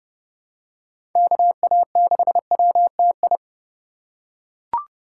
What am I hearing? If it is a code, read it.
Morse code, heard as a keyed tone, about 30 wpm, 700 Hz. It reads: KA6WTI
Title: USA Call Signs - 2x Character Spacing - 30wpm